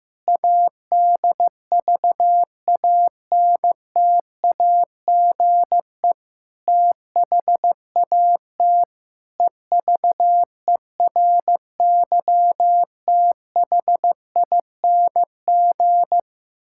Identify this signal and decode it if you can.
Morse code, heard as a keyed tone, about 14 wpm, 700 Hz. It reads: ADVANTAGE THAT EVERYTHING
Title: Sentences from Top 1000 Words - Review Part 6 - 15wpm